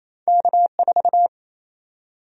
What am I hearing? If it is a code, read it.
Morse code, heard as a keyed tone, about 27 wpm, 700 Hz. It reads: K4
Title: USA Call Sign Prefixes - 28wpm